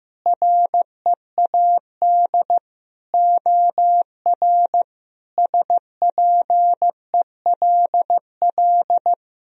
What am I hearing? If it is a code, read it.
Morse code, heard as a keyed tone, about 15 wpm, 700 Hz. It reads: READ OR SPELL
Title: Sentences from Top 1000 Words - Review Part 3 - 15wpm